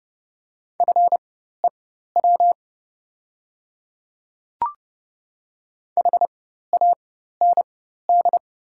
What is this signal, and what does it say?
Morse code, heard as a keyed tone, about 29 wpm, spaced as for 14 wpm, 700 Hz. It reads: FEW HAND
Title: 4x Character Spacing - Top 300 Words - 30wpm